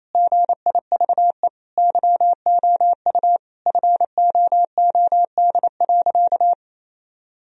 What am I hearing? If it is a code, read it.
Morse code, heard as a keyed tone, about 28 wpm, 700 Hz. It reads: GIVE YOU FOOD.
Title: Sentences from Top 1000 Words - Review Part 1 - 28wpm